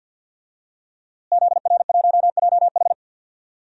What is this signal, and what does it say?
Morse code, heard as a keyed tone, about 47 wpm, 700 Hz. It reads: ZR1JH
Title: DXCC-100 Call Signs - 5 Characters - Repeated 2x - 50wpm